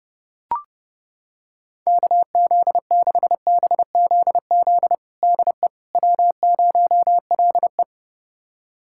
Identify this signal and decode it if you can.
Morse code, heard as a keyed tone, about 30 wpm, 700 Hz. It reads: KZ6BZZ DE W0LE
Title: USA Callsign DE Callsign 30wpm